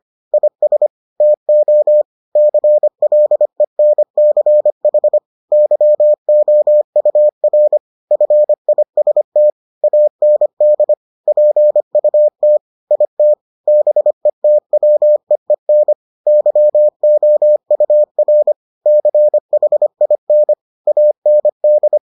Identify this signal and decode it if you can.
Morse code, heard as a keyed tone, about 25 wpm, 600 Hz. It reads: IS TO CLENCH YOUR FIST AND PUT IT BETWEEN YOUR CHIN AND